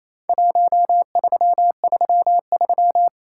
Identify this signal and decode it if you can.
Morse code, heard as a keyed tone, about 28 wpm, 700 Hz. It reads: 1333
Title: Worked All Europe DX Contest - QTC Exchange - 28wpm